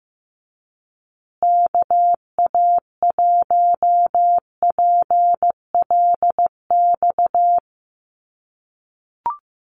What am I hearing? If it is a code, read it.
Morse code, heard as a keyed tone, about 15 wpm, 700 Hz. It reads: KA1PLX
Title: USA Call Signs Repeated 3x 15wpm